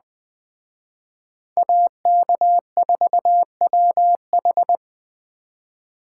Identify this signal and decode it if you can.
Morse code, heard as a keyed tone, about 20 wpm, 700 Hz. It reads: AK4WH